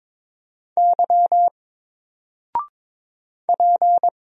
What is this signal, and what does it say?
Morse code, heard as a keyed tone, about 22 wpm, 700 Hz. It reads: Y P
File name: Single-Letter-Number-22wpm